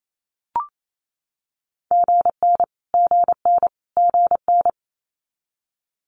Morse code, heard as a keyed tone, about 28 wpm, 700 Hz. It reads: GN GN GN
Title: Multi-letter Phonemes Repeated 3x 28wpm